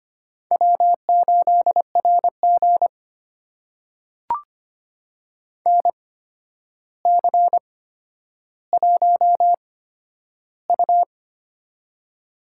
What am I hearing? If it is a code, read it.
Morse code, heard as a keyed tone, about 25 wpm, spaced as for 11 wpm, 700 Hz. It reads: W8RG NC1U
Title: USA Call Signs - 4 Characters - 8x Character Spacing - 25wpm